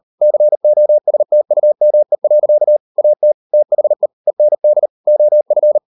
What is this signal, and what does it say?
Morse code, heard as a keyed tone, about 38 wpm, 600 Hz. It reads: COSTUME. AT THE END OF